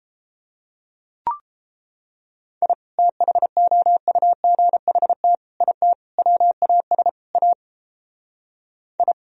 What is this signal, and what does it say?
Morse code, heard as a keyed tone, about 29 wpm, 700 Hz. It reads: I THOUGHT IT WAS A I